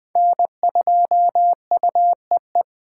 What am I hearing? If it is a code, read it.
Morse code, heard as a keyed tone, about 21 wpm, 700 Hz. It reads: N2UEE